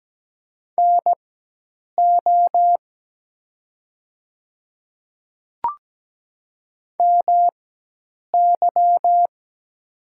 Morse code, heard as a keyed tone, about 17 wpm, spaced as for 8 wpm, 700 Hz. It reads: NO MY
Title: 4x Character Spacing - Top 300 Words - 17wpm